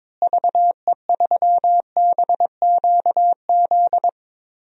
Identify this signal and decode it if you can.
Morse code, heard as a keyed tone, about 22 wpm, 700 Hz. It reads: VE3BQZ
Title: DXCC-100 Call Signs - 6 Characters - Repeated 3x - 22wpm